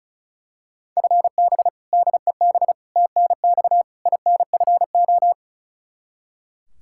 Morse code, heard as a keyed tone, about 36 wpm, 700 Hz. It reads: FB DEB TNX INFO
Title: Practice QSO - Exchange 3 - More Info - 35wpm